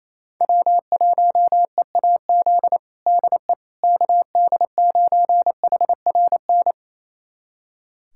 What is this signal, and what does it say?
Morse code, heard as a keyed tone, about 28 wpm, 700 Hz. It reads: W1EAZ DE KD9HRN
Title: USA Callsign DE Callsign 28wpm